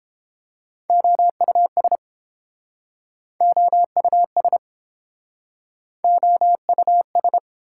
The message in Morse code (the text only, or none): OUS OUS OUS